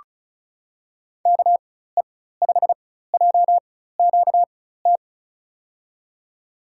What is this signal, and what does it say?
Morse code, heard as a keyed tone, about 36 wpm, spaced as for 16 wpm, 700 Hz. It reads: KE5JQT
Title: USA Call Signs - 6 Characters - 4x Character Spacing - 35wpm